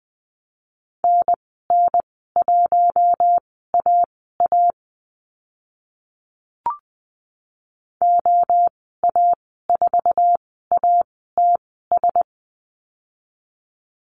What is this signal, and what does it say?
Morse code, heard as a keyed tone, about 20 wpm, spaced as for 15 wpm, 700 Hz. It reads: NN1AA OA4ATS
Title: 062 Lesson 11g - Letter h - Call Signs 20wpm